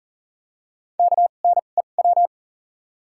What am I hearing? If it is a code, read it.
Morse code, heard as a keyed tone, about 40 wpm, spaced as for 29 wpm, 700 Hz. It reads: KNEW
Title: 2x Character Spacing - Top 500 Words - 40wpm